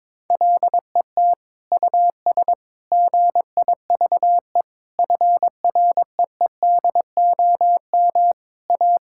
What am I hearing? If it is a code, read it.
Morse code, heard as a keyed tone, about 22 wpm, 700 Hz. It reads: LET US GIVE FREEDOM A
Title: Sentences from Top 1000 Words - Review Part 5 - 22wpm